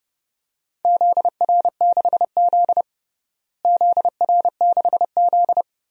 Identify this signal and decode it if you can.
Morse code, heard as a keyed tone, about 32 wpm, 700 Hz. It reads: ZR6Z ZR6Z
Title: DXCC-100 Call Signs - 4 Characters - Repeated 2x - 30wpm